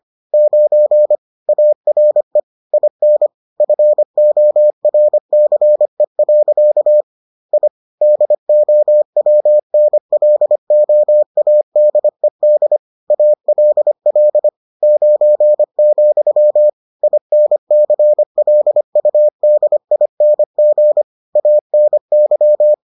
Morse code, heard as a keyed tone, about 25 wpm, 600 Hz. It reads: 9 ARE IN FORCE. I DOWNLOADED ALL 9, INCLUDING ANY